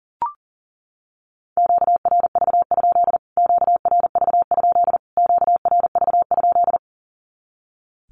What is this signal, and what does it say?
Morse code, heard as a keyed tone, about 40 wpm, 700 Hz. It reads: QRV? QRV? QRV?
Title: Q-Codes Repeated 3x 40wpm